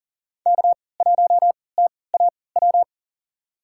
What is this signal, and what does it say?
Morse code, heard as a keyed tone, about 39 wpm, spaced as for 23 wpm, 700 Hz. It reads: K1TAW
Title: USA Call Signs - 5 Characters - 3x Character Spacing - 40wpm